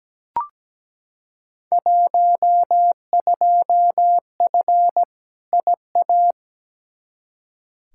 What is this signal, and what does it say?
Morse code, heard as a keyed tone, about 17 wpm, 700 Hz. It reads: 12F IA
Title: Field Day Exchanges Encore - 17wpm